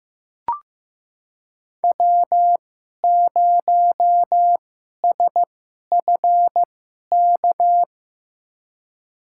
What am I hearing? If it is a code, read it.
Morse code, heard as a keyed tone, about 15 wpm, spaced as for 11 wpm, 700 Hz. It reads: W0SFK
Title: USA Call Signs - 5 Characters - 2x Character Spacing - 15wpm